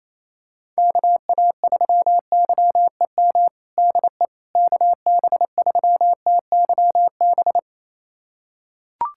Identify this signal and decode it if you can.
Morse code, heard as a keyed tone, about 28 wpm, 700 Hz. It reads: KA3YEM DE KB3TYB
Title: USA Callsign DE Callsign 28wpm